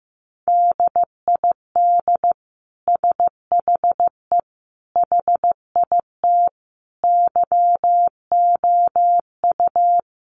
Morse code, heard as a keyed tone, about 15 wpm, 700 Hz. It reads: DID SHE HIT YOU
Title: Sentences from Top 1400 Words 15wpm